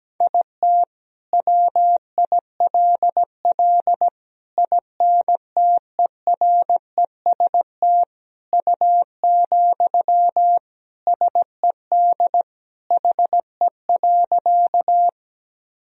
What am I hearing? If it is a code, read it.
Morse code, heard as a keyed tone, about 17 wpm, 700 Hz. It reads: IT WILL INTEREST U, SED HE.
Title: Sentences from Top 400 Words and Abbreviations 17wpm